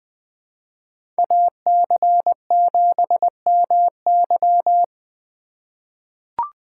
Morse code, heard as a keyed tone, about 20 wpm, 700 Hz. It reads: AC7MY
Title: USA Call Signs - 5 Characters - 4x Character Spacing - 20wpm